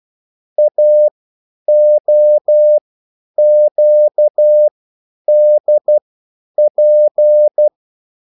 Morse code, heard as a keyed tone, about 12 wpm, spaced as for 9 wpm, 600 Hz. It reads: AOQDP